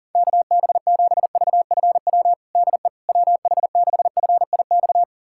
Morse code, heard as a keyed tone, about 40 wpm, 700 Hz. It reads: KB7VFW DE WH6FIX
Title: USA Callsign DE Callsign - 6 Characters - 40wpm